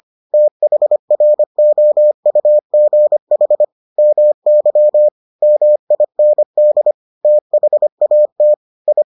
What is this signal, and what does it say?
Morse code, heard as a keyed tone, about 25 wpm, 600 Hz. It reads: THROUGH MY MIND THAT I